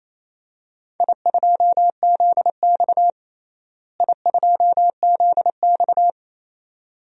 Morse code, heard as a keyed tone, about 28 wpm, 700 Hz. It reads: I2ZX I2ZX
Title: DXCC-100 Call Signs - 4 Characters - Repeated 2x - 28wpm